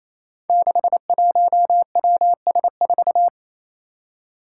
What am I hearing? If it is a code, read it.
Morse code, heard as a keyed tone, about 28 wpm, 700 Hz. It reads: 61WS4
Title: Five Challenging Letter-Number Combinations 28wpm